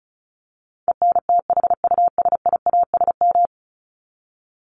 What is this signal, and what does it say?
Morse code, heard as a keyed tone, about 34 wpm, 700 Hz. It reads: ENTHUSIASM
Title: Top 1751-2000 Words 35wpm